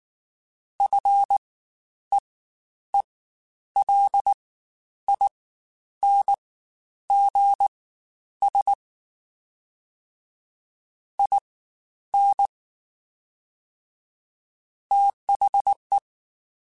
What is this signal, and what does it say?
Morse code, heard as a keyed tone, about 19 wpm, spaced as for 10 wpm, 800 Hz. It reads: FEELINGS IN THE
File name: Quotes Spaced / Quotes Spaced-mp3s / ChildAsleep sp @18